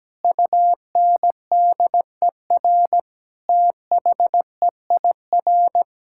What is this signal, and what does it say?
Morse code, heard as a keyed tone, about 17 wpm, 700 Hz. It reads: UNDER THEIR
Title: Sentences from Top 1600 Words 17wpm